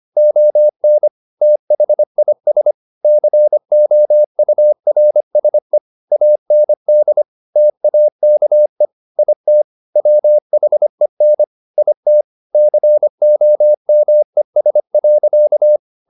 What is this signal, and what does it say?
Morse code, heard as a keyed tone, about 25 wpm, 600 Hz. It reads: ON THIS COURSE AND TAKE IT WHEN IT COMES.